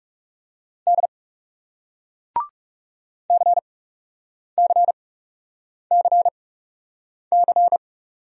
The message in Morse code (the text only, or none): D CCCC